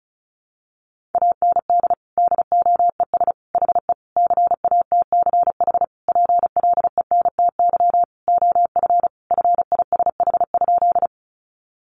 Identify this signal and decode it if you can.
Morse code, heard as a keyed tone, about 34 wpm, 700 Hz. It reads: AND DOES HE CATCH PLENTY OF FISH?
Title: Sentences from Top 1500 Words 35wpm